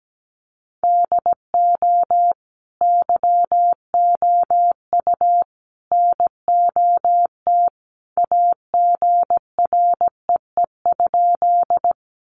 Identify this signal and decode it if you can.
Morse code, heard as a keyed tone, about 17 wpm, 700 Hz. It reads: DO YOU NOT AGREE?
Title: Sentences from Top 1100 Words 17wpm